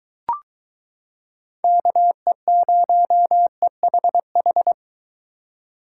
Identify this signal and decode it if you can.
Morse code, heard as a keyed tone, about 23 wpm, 700 Hz. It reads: KE0EHH